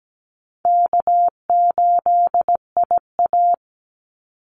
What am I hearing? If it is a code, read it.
Morse code, heard as a keyed tone, about 17 wpm, 700 Hz. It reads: K8IA
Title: Easy Call Signs - 17wpm